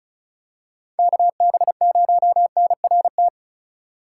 Morse code, heard as a keyed tone, about 34 wpm, 700 Hz. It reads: KB0NRT
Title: USA Call Signs - 6 Characters - 2x Character Spacing - 35wpm